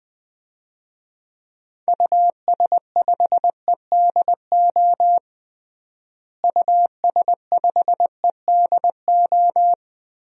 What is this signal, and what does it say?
Morse code, heard as a keyed tone, about 20 wpm, 700 Hz. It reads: US5EDO US5EDO
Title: DXCC-100 Call Signs - 6 Characters - Repeated 2x - 20wpm